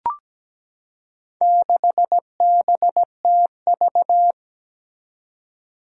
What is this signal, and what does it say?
Morse code, heard as a keyed tone, about 17 wpm, 700 Hz. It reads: 6BTV
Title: Four Challenging Letter-Number Combinations 17wpm